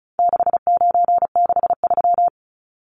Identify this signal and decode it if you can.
Morse code, heard as a keyed tone, about 34 wpm, 700 Hz. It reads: 6963
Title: Numbers - 4-Digit 35wpm